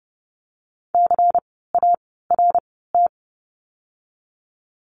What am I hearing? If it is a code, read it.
Morse code, heard as a keyed tone, about 30 wpm, 700 Hz. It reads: C A R T
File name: Lesson 16f - Letter c - Words 30wpm